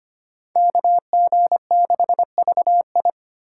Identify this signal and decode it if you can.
Morse code, heard as a keyed tone, about 25 wpm, 700 Hz. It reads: KG6VI
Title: USA Callsign DE Callsign - 5 Characters - 25wpm